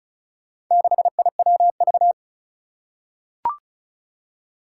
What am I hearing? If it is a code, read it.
Morse code, heard as a keyed tone, about 36 wpm, 690 Hz. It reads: 6IWV
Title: Four Challenging Letter-Number Combinations 35wpm